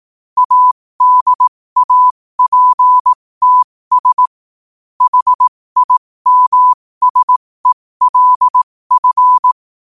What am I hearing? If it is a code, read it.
Morse code, heard as a keyed tone, about 18 wpm, spaced as for 15 wpm, 1000 Hz. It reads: ADAPTS HIMSELF
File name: qotdfiles / 15wpm